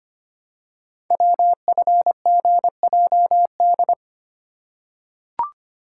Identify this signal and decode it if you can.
Morse code, heard as a keyed tone, about 25 wpm, 700 Hz. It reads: WFGJD
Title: Code Groups - 5 Letter:Numbers - 25wpm